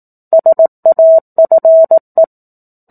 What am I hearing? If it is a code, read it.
Morse code, heard as a keyed tone, about 18 wpm, 650 Hz. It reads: SAFE
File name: 04-Chapter-04 18 WPM